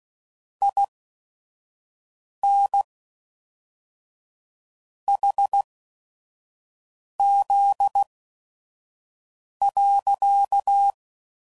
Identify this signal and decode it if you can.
Morse code, heard as a keyed tone, about 16 wpm, spaced as for 5 wpm, 800 Hz. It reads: IN HZ.
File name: CW Code Groups @5WPM / Technical Subjects / XL